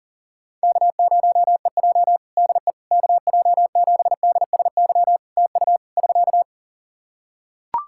Morse code, heard as a keyed tone, about 40 wpm, 700 Hz. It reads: K0EJ DE KJ7DSY TU <SK>
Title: Practice QSO - Exchange 4 - Ending - 40wpm